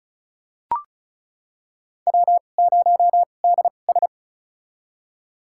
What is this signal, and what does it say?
Morse code, heard as a keyed tone, about 36 wpm, spaced as for 25 wpm, 700 Hz. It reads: W0DS
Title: USA Call Signs - 2x Character Spacing - 35wpm